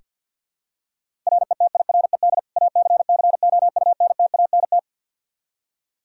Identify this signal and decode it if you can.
Morse code, heard as a keyed tone, about 47 wpm, 700 Hz. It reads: RETIRED ACCOUNTANT
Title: Practice QSO - Exchange 3 - More Info - 50wpm